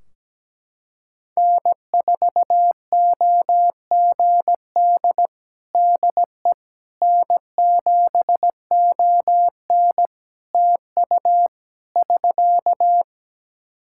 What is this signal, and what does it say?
Morse code, heard as a keyed tone, about 17 wpm, 700 Hz. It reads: N4OGD DE N7ON TU <SK>
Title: Practice QSO - Exchange 4 - Ending - 17wpm